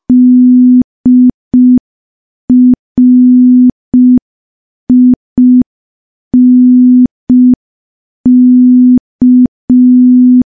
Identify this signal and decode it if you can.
Morse code, heard as a keyed tone, about 5 wpm, 250 Hz. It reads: DRINK
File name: morse / morse-training-05WPM